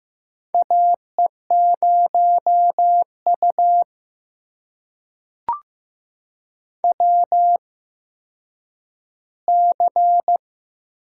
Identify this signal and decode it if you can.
Morse code, heard as a keyed tone, about 15 wpm, spaced as for 9 wpm, 700 Hz. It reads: AE0U WC
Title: USA Call Signs - 4 Characters - 8x Character Spacing - 15wpm